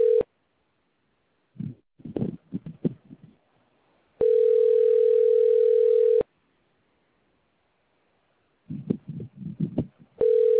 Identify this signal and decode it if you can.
Ringback tone (North America), 440 + 480 Hz.